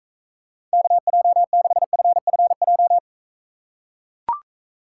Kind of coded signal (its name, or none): Morse code